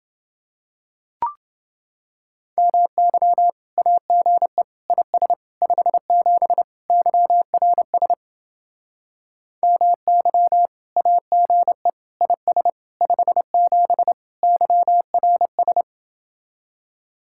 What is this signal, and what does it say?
Morse code, heard as a keyed tone, about 29 wpm, 700 Hz. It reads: MY AGE IS 57 YRS MY AGE IS 57 YRS